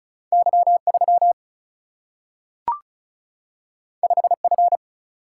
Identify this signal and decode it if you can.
Morse code, heard as a keyed tone, about 36 wpm, 700 Hz. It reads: Y 3 5 F
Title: Single-Letter Single-Number 35wpm